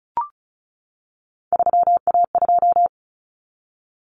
Morse code, heard as a keyed tone, about 34 wpm, 700 Hz. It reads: 3A2
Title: Common Cut Numbers - 3-Digit 35wpm